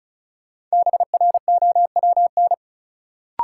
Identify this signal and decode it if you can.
Morse code, heard as a keyed tone, about 36 wpm, 700 Hz. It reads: BROWN
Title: Sentences from Top 1200 Words - New Word Drill Alt - 35wpm